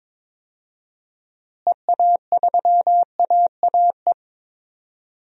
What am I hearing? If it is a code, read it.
Morse code, heard as a keyed tone, about 22 wpm, 700 Hz. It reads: EA3AAE
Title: DXCC-100 Call Signs - 6 Characters - Repeated 3x - 22wpm